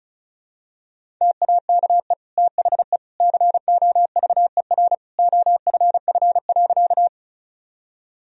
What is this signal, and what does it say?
Morse code, heard as a keyed tone, about 36 wpm, 700 Hz. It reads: TAKE THE COVER OFF.
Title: Sentences from Top 1000 Words - Review Part 1 - 35wpm